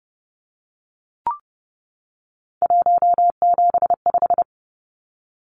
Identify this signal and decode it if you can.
Morse code, heard as a keyed tone, about 30 wpm, 700 Hz. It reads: 175
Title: Numbers - 3-Digit 30wpm